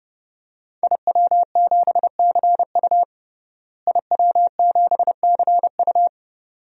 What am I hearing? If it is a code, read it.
Morse code, heard as a keyed tone, about 30 wpm, 700 Hz. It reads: IW7CU IW7CU
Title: DXCC-100 Call Signs - 5 Characters - Repeated 2x - 30wpm